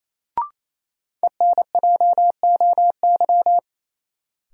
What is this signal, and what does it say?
Morse code, heard as a keyed tone, about 28 wpm, 700 Hz. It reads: ENJOY
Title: Sentences from Top 1000 Words - Review Part 4 - New Word Drill - 28wpm